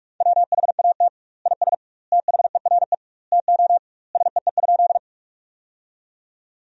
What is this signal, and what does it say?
Morse code, heard as a keyed tone, about 46 wpm, 700 Hz. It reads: WHAT IS THERE TO SEE?
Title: 2x Word-Spacing - Sentences from Top 100 Words - 45wpm